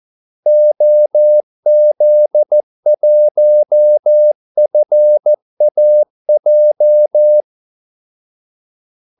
Morse code, heard as a keyed tone, about 14 wpm, 600 Hz. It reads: OZ1FAJ
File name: Callsigns-14wpm